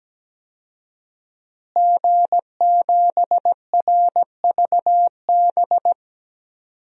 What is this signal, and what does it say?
Morse code, heard as a keyed tone, about 17 wpm, 700 Hz. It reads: G7RVB
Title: DXCC-100 Call Signs - 5 Characters - Repeated 2x - 17wpm